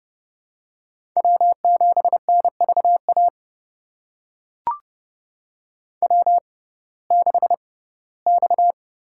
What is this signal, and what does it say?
Morse code, heard as a keyed tone, about 30 wpm, spaced as for 18 wpm, 700 Hz. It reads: W7NVA W6X
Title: USA Call Signs - 5 Characters - 6x Character Spacing - 30wpm